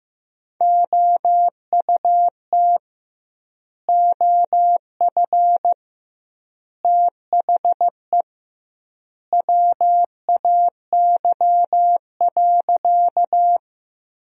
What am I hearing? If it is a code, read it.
Morse code, heard as a keyed tone, about 15 wpm, 700 Hz. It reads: OUT OF THE WAY.
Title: 2x Word-Spacing - Sentences from Top 100 Words - 15wpm